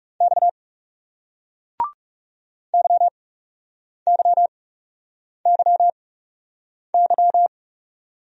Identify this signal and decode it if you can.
Morse code, heard as a keyed tone, about 40 wpm, spaced as for 8 wpm, 700 Hz. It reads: X YYYY